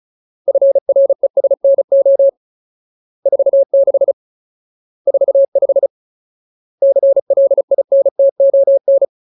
Morse code, heard as a keyed tone, about 34 wpm, 550 Hz. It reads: FRESNO 46 45 CLINTON
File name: Ultimate Word List / 35WPM Ultimate